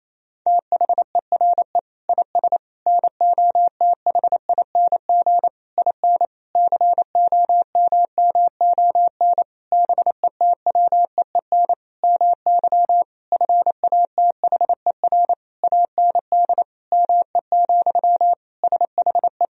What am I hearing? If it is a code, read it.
Morse code, heard as a keyed tone, about 27 wpm, 700 Hz. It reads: THERE IS NOTHING IN COMMON BETWEEN MY FATHER AND ME, SHE
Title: Sentences from Top 1000 Words - Review Part 3 - 28wpm